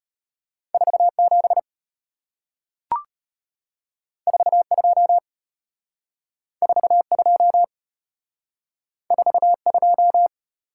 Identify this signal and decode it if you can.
Morse code, heard as a keyed tone, about 34 wpm, 690 Hz. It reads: 47 42 42 42